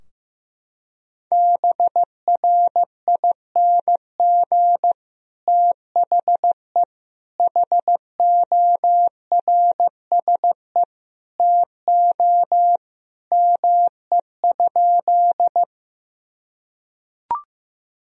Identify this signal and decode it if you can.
Morse code, heard as a keyed tone, about 15 wpm, 700 Hz. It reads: BRING THE HORSE TO ME?
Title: Sentences from Top 1000 Words - Review Part 3 - 15wpm